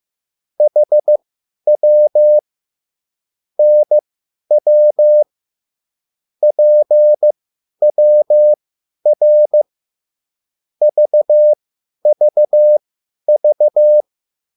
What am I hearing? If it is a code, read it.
Morse code, heard as a keyed tone, about 15 wpm, spaced as for 10 wpm, 600 Hz. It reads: HW NW PWR VVV